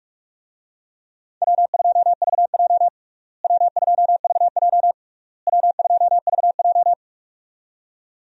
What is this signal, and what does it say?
Morse code, heard as a keyed tone, about 41 wpm, 700 Hz. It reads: W2VJ W2VJ W2VJ
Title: USA Call Signs - 4 Characters - Repeated 3x - 45wpm